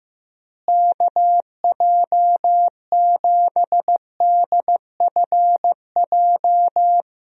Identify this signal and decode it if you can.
Morse code, heard as a keyed tone, about 15 wpm, 700 Hz. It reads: KJ7DFJ
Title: USA Call Signs - 6 Characters - Repeated 2x - 15wpm